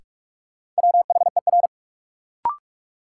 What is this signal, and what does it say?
Morse code, heard as a keyed tone, about 46 wpm, 700 Hz. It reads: WHER
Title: Sentences from Top 1500 Words - New Word Drill - 45wpm